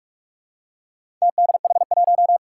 Morse code, heard as a keyed tone, about 46 wpm, 700 Hz. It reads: TDH1
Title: Four Challenging Letter-Number Combinations 45wpm